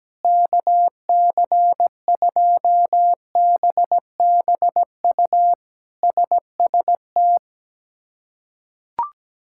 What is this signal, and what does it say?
Morse code, heard as a keyed tone, about 17 wpm, 700 Hz. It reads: KC2BBU SST
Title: K1USN SST QSO Practice - 17wpm